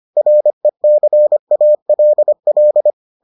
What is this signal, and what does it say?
Morse code, heard as a keyed tone, about 25 wpm, 600 Hz. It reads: RECALL